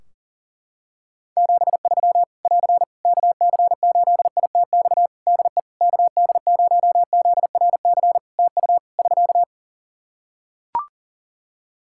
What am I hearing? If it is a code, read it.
Morse code, heard as a keyed tone, about 40 wpm, 700 Hz. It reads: 73 <AR> KC8ITX DE KD0ZRC TU <SK>
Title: Practice QSO - Exchange 4 - Ending - 40wpm